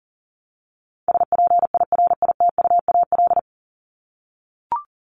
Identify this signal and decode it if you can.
Morse code, heard as a keyed tone, about 40 wpm, 700 Hz. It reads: SPIRITUAL
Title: Top 1501-1750 Words 40wpm